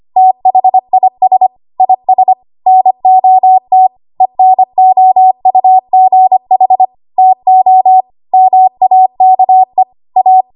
Morse code, heard as a keyed tone, about 25 wpm, 750 Hz. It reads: THIS IS NOT ENOUGH TO MAKE A
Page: W1AW Web Code Practice Files